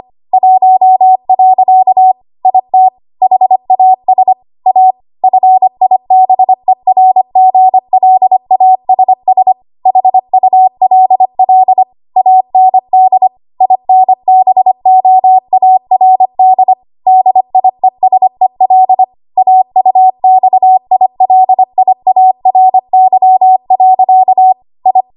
Morse code, heard as a keyed tone, about 25 wpm, 750 Hz. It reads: 1. IT HAS A FIBERGLASS HULL AND INBOARD DIESEL AUXILIARY. I
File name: Archive / 25 WPM / 25WPM